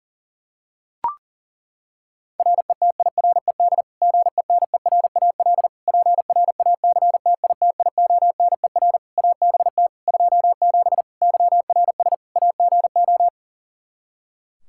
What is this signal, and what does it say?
Morse code, heard as a keyed tone, about 40 wpm, 700 Hz. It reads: RETIRED GENERAL PRACTITIONER ABT 27 YRS AGO
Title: Practice QSO - Exchange 3 - More Info - 40wpm